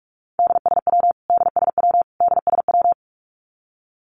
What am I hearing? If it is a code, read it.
Morse code, heard as a keyed tone, about 44 wpm, 700 Hz. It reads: DSW DSW DSW
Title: Abbreviations Repeated 3x - 45wpm